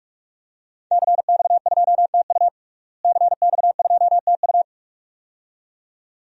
Morse code, heard as a keyed tone, about 41 wpm, 700 Hz. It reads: CX2TU CX2TU
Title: DXCC-100 Call Signs - 5 Characters - Repeated 2x - 45wpm